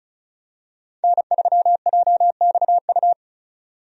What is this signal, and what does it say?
Morse code, heard as a keyed tone, about 34 wpm, 700 Hz. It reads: N3JXU
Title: USA Call Signs - 5 Characters - 35wpm